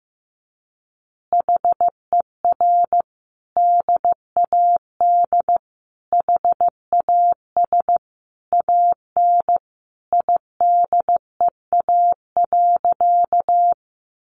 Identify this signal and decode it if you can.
Morse code, heard as a keyed tone, about 15 wpm, 700 Hz. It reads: HER DAD HAS AN IDEA.
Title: Easy Phrases 15wpm